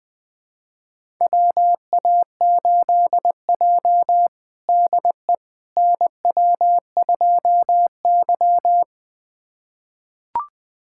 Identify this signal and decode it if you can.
Morse code, heard as a keyed tone, about 20 wpm, 700 Hz. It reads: WA8J DE NW2Y
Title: USA Callsign DE Callsign - 4 Characters - 20wpm